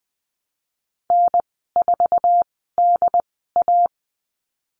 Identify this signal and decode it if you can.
Morse code, heard as a keyed tone, about 19 wpm, spaced as for 14 wpm, 700 Hz. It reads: N4DA
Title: 088 Lesson 15g - Number 5 - Call Signs 20wpm